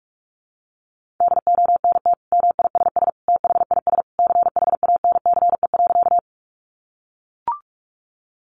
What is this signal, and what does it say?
Morse code, heard as a keyed tone, about 44 wpm, 700 Hz. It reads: DONT MISS THIS CHANCE.
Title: Sentences from Top 800 Words 45wpm